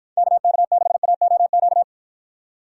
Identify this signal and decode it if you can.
Morse code, heard as a keyed tone, about 49 wpm, 700 Hz. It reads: KK6AOQ